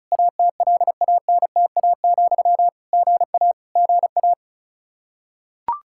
Morse code, heard as a keyed tone, about 36 wpm, 700 Hz. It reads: ATLANTA, GA GA
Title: Practice QSO - Exchange 1 - Beginning - 35wpm